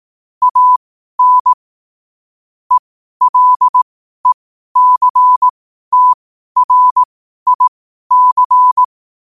Morse code, heard as a keyed tone, about 18 wpm, spaced as for 12 wpm, 1000 Hz. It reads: AN ELECTRIC
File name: qotdfiles / 12wpm